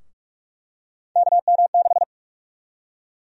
Morse code, heard as a keyed tone, about 42 wpm, 700 Hz. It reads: KM6
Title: USA Call Sign Prefixes - 45wpm